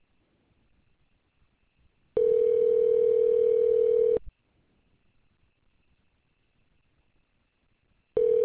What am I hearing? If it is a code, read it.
Ringback tone (North America), 440 + 480 Hz.